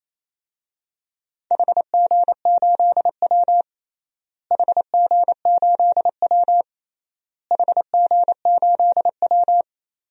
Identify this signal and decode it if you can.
Morse code, heard as a keyed tone, about 28 wpm, 700 Hz. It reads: HG8W HG8W HG8W
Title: DXCC-100 Call Signs - 4 Characters - Repeated 3x - 28wpm